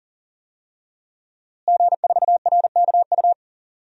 Morse code, heard as a keyed tone, about 40 wpm, 700 Hz. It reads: G4RKU
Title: DXCC-100 Call Signs - 5 Characters - Repeated 2x - 40wpm